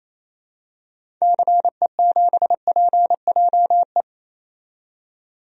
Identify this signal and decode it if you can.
Morse code, heard as a keyed tone, about 28 wpm, 700 Hz. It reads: CE7PJE
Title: DXCC-100 Call Signs - 6 Characters - 28wpm